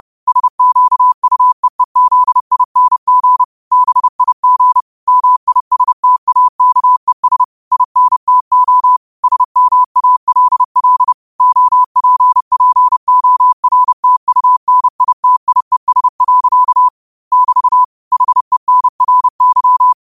Morse code, heard as a keyed tone, about 30 wpm, 1000 Hz. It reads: SQUEEZING BIG MISTAKES INTO SMALL OPPORTUNITIES. <BT> HENRY